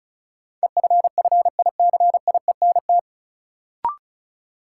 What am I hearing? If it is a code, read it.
Morse code, heard as a keyed tone, about 34 wpm, 700 Hz. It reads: EFFICIENT
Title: Sentences from Top 1300 Words - New Word Drill - 35wpm